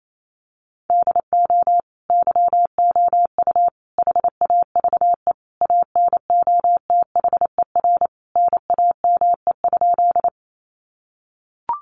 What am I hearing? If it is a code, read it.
Morse code, heard as a keyed tone, about 28 wpm, 700 Hz. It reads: DO YOU HAVE ANOTHER NAME?
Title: Sentences from Top 600 Words 28wpm